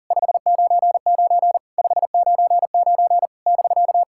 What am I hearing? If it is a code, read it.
Morse code, heard as a keyed tone, about 40 wpm, 690 Hz. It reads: 599 599 <BK>
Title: SOTA QSO Practice 40wpm